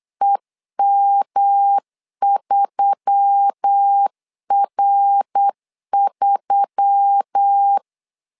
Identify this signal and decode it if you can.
Morse code, heard as a keyed tone, about 8 wpm, 800 Hz. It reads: EM3R3
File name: LEZIONI / 4 Q1S3 8wpm 3037